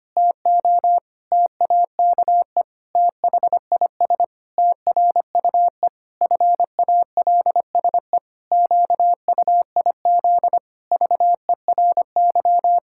Morse code, heard as a keyed tone, about 25 wpm, 700 Hz. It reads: TO TAKE THIS TRUE FALSE QUIZ VERY
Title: Pangram Challenge v2 - 25wpm